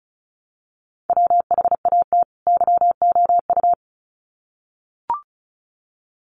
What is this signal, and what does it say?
Morse code, heard as a keyed tone, about 34 wpm, 700 Hz. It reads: WHAT YOU
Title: Sets of 2 Words 35wpm